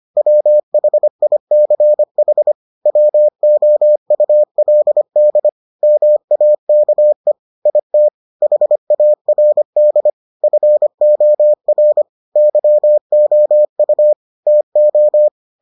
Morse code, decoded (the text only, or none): WHICH WOULD MAKE IT HARD FOR YOU TO